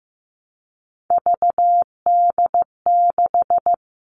Morse code, heard as a keyed tone, about 15 wpm, 700 Hz. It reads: VD6
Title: Three Challenging Letter-Number Combinations 15wpm